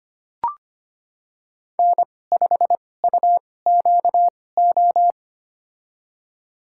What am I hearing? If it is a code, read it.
Morse code, heard as a keyed tone, about 25 wpm, spaced as for 18 wpm, 700 Hz. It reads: N5UQO
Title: USA Call Signs - 2x Character Spacing - 25wpm